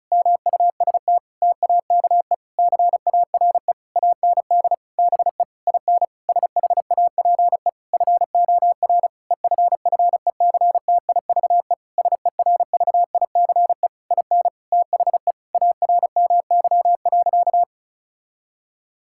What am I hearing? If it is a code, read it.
Morse code, heard as a keyed tone, about 34 wpm, 700 Hz. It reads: MUST TAKE CARE AND BE IN SHAPE FOR EFFECTIVE SERVICE IN THE ARMY.
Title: Sentences from Top 1000 Words - Review Part 6 - 35wpm